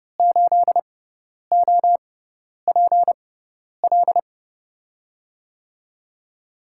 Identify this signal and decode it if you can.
Morse code, heard as a keyed tone, about 30 wpm, spaced as for 10 wpm, 700 Hz. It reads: 8OPL
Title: USA Call Signs - 6 Characters - 6x Character Spacing - 30wpm